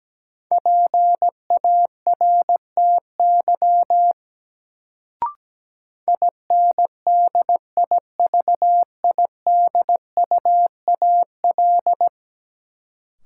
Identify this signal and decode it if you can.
Morse code, heard as a keyed tone, about 17 wpm, 700 Hz. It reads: PARTY INDIVIDUAL
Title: Sentences from Top 1400 Words - New Word Drill Alt - 17wpm